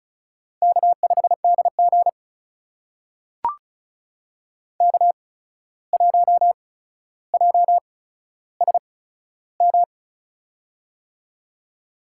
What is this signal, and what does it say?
Morse code, heard as a keyed tone, about 34 wpm, spaced as for 14 wpm, 700 Hz. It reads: K5DG K1JSM
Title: USA Call Signs - 8x Character Spacing - 35wpm